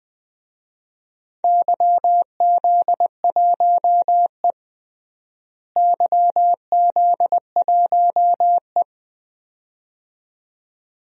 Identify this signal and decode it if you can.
Morse code, heard as a keyed tone, about 20 wpm, 700 Hz. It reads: YZ1E YZ1E
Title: DXCC-100 Call Signs - 4 Characters - Repeated 2x - 20wpm